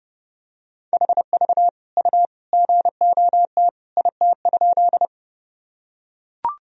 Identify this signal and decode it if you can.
Morse code, heard as a keyed tone, about 32 wpm, 700 Hz. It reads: HV U GOT IT?
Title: Sentences from Top 300 Words and Abbreviations 30wpm